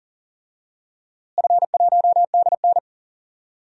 Morse code, heard as a keyed tone, about 40 wpm, 700 Hz. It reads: F1DN
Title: DXCC-100 Call Signs - 4 Characters - Repeated 3x - 40wpm